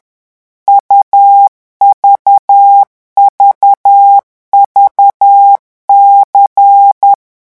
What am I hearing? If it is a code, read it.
Morse code, heard as a keyed tone, about 10 wpm, 800 Hz. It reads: UVVVC
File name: morse mp3 / revision2-vitesse 10 mots